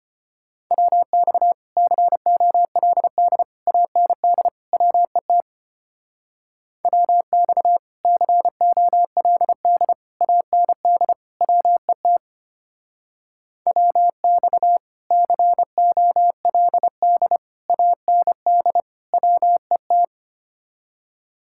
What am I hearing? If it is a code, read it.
Morse code, heard as a keyed tone, about 31 wpm, 700 Hz. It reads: WX COLD AND WET WX COLD AND WET WX COLD AND WET
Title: Speed-Racing Common QSO Phrases - 25wpm